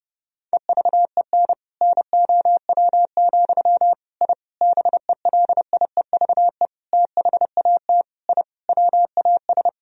Morse code, heard as a keyed tone, about 32 wpm, 700 Hz. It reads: EVEN NOW, I BELIEVE THAT I WAS
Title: Sentences from Top 1000 Words - Review Part 1 - 30wpm